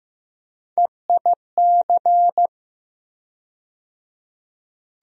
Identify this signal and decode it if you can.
Morse code, heard as a keyed tone, about 15 wpm, 700 Hz. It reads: EIC
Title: USA Call Sign Postfixes - 3 Letters 15wpm